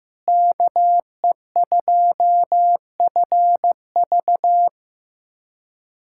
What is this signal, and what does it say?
Morse code, heard as a keyed tone, about 15 wpm, 700 Hz. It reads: KE2FV
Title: USA Callsign DE Callsign - 5 Characters - 15wpm